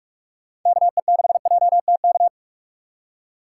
Morse code, heard as a keyed tone, about 46 wpm, 700 Hz. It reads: KE6JTK
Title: USA Call Signs - 6 Characters - 6x Character Spacing - 45wpm